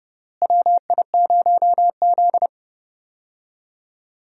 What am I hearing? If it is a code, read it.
Morse code, heard as a keyed tone, about 30 wpm, 700 Hz. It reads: WI0Z
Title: USA Call Signs - 4 Characters - Repeated 3x - 30wpm